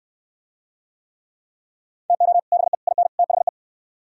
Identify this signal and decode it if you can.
Morse code, heard as a keyed tone, about 63 wpm, 700 Hz. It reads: TO BE IN THE